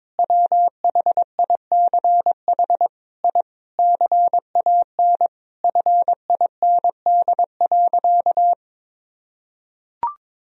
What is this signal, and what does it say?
Morse code, heard as a keyed tone, about 22 wpm, 700 Hz. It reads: WHICH I CAN FIND.
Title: Sentences from Top 1000 Words - Review Part 2 - 22wpm